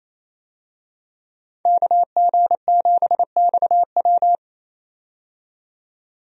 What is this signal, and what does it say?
Morse code, heard as a keyed tone, about 27 wpm, 700 Hz. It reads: KG7XW
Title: USA Call Signs - 5 Characters - 28wpm